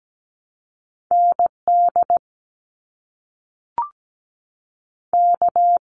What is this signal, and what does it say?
Morse code, heard as a keyed tone, about 17 wpm, 700 Hz. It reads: ND K
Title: US States 17wpm